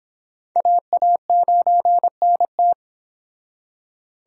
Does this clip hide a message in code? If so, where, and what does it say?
Yes: Morse code from 0.6 to 2.7 s, heard as a keyed tone, about 30 wpm, 700 Hz. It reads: AA9NT